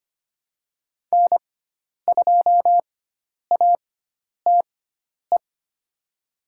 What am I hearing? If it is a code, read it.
Morse code, heard as a keyed tone, about 25 wpm, spaced as for 10 wpm, 700 Hz. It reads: N2ATE
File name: Lesson 16g - Letter c - Call Signs 25wpm